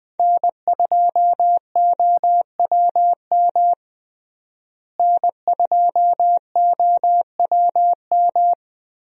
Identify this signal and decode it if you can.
Morse code, heard as a keyed tone, about 20 wpm, 700 Hz. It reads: N2OWM N2OWM
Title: USA Call Signs - 5 Characters - Repeated 3x - 20wpm